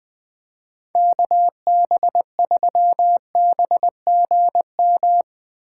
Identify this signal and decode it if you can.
Morse code, heard as a keyed tone, about 20 wpm, 700 Hz. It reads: KB3BGM